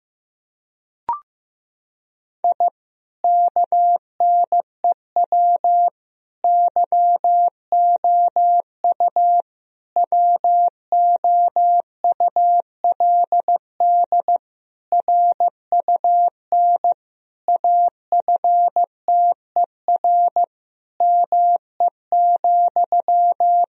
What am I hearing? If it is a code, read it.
Morse code, heard as a keyed tone, about 15 wpm, 700 Hz. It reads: I KNEW YOU WOULD RUN AFTER ME,
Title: Sentences from Top 1000 Words - Review Part 1 - 15wpm